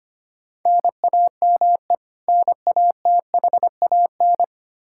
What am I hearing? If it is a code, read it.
Morse code, heard as a keyed tone, about 25 wpm, 700 Hz. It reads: NAME NATHAN
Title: Practice QSO - Exchange 1 - Beginning - 25wpm